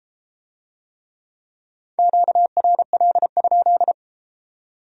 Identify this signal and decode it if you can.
Morse code, heard as a keyed tone, about 33 wpm, 700 Hz. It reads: QRL?